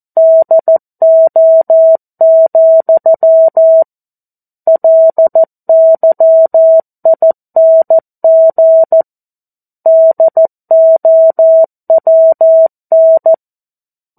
Morse code, heard as a keyed tone, about 14 wpm, 650 Hz. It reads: DO, LYING DOWN
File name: 02-Chapter-02 14 WPM